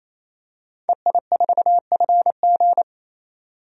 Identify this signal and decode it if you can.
Morse code, heard as a keyed tone, about 28 wpm, 700 Hz. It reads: EI4FG
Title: DXCC-100 Call Signs - 5 Characters - 28wpm